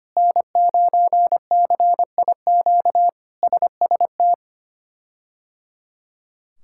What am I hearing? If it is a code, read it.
Morse code, heard as a keyed tone, about 25 wpm, 700 Hz. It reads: N9CIQ SST
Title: K1USN SST QSO Practice - 25wpm